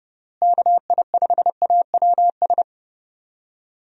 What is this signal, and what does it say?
Morse code, heard as a keyed tone, about 32 wpm, 700 Hz. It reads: KI5AWS
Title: USA Call Signs - 6 Characters - Repeated 2x - 30wpm